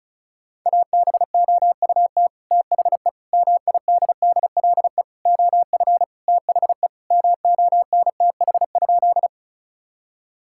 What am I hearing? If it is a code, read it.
Morse code, heard as a keyed tone, about 34 wpm, 700 Hz. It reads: ABOUT THE MIDDLE OF THE MONTH?
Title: Sentences from Top 1000 Words - Review Part 6 - 35wpm